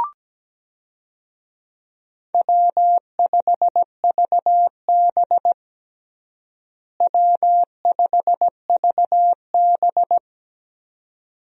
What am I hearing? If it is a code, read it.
Morse code, heard as a keyed tone, about 17 wpm, 700 Hz. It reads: W5VB W5VB
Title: USA Call Signs - 4 Characters - Repeated 2x - 17wpm